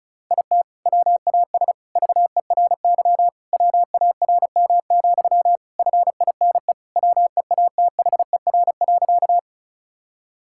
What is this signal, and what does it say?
Morse code, heard as a keyed tone, about 34 wpm, 690 Hz. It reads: IT WAS VERY WARM, FINE WEATHER.
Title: Sentences from Top 1000 Words - Review Part 5 - 35wpm